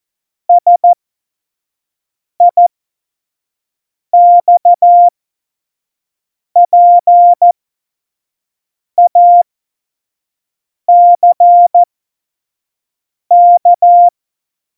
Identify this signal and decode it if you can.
Morse code, heard as a keyed tone, about 14 wpm, spaced as for 5 wpm, 700 Hz. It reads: SIXPACK
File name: Morsekurs / Witze / Witz 11. Februar 2020, 25 F70